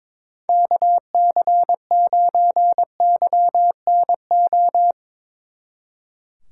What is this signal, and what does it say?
Morse code, heard as a keyed tone, about 25 wpm, 700 Hz. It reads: KC9YNO
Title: Speed Racing - USA 2x3 Call Signs 22wpm